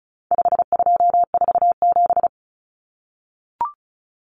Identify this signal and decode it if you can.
Morse code, heard as a keyed tone, about 34 wpm, 700 Hz. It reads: 5247
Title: Common Cut Numbers - 4-Digit 35wpm